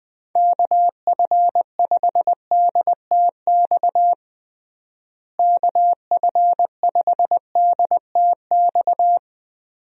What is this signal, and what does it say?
Morse code, heard as a keyed tone, about 20 wpm, 700 Hz. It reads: KF5DTX KF5DTX
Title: USA Call Signs - 6 Characters - Repeated 3x - 20wpm